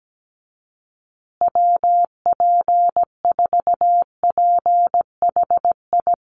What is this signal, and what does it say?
Morse code, heard as a keyed tone, about 17 wpm, 700 Hz. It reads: WP4PHI
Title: USA Call Signs Repeated 3x 17wpm